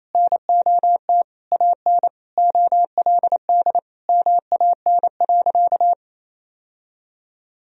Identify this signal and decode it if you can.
Morse code, heard as a keyed tone, about 28 wpm, 700 Hz. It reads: NOT AN OLD MAN.
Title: Sentences from Top 1000 Words - Review Part 4 - 28wpm